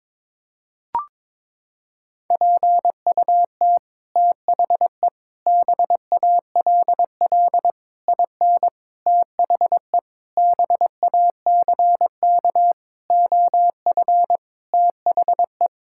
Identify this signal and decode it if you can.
Morse code, heard as a keyed tone, about 22 wpm, 700 Hz. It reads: PUT THE BALL IN THE BACK OF THE
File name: Sets of 9 Words - 22wpm